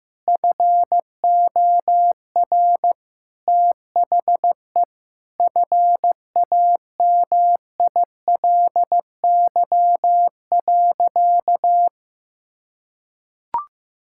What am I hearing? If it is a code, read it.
Morse code, heard as a keyed tone, about 15 wpm, 700 Hz. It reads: FOR THE FAMILY.
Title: Sentences from Top 1000 Words - Review Part 5 - 15wpm